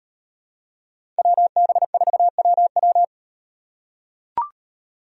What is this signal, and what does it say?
Morse code, heard as a keyed tone, about 38 wpm, 700 Hz. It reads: WB4WW